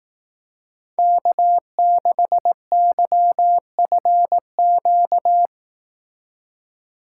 Morse code, heard as a keyed tone, about 17 wpm, 700 Hz. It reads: K6YFQ
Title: Speed Racing - USA 1x3 Call Signs 15wpm